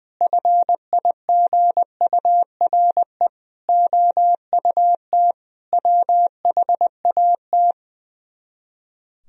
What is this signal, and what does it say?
Morse code, heard as a keyed tone, about 20 wpm, 700 Hz. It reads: FIGURE OUT WHAT
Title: Speed-Racing - Sets of 5 Words 20wpm